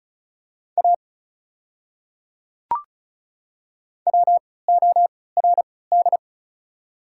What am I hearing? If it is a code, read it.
Morse code, heard as a keyed tone, about 36 wpm, spaced as for 20 wpm, 690 Hz. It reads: A WORD
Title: 3x Character Spacing - Top 100 Words - 35wpm